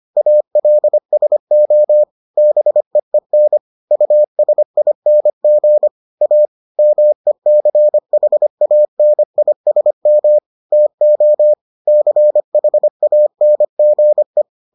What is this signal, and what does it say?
Morse code, heard as a keyed tone, about 25 wpm, 600 Hz. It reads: ALSO BEEN USING A MECHANISM TO CHANGE